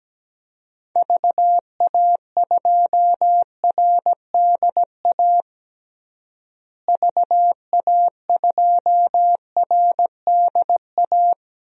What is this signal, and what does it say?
Morse code, heard as a keyed tone, about 17 wpm, 700 Hz. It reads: VA2RDA VA2RDA
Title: DXCC-100 Call Signs - 6 Characters - Repeated 3x - 17wpm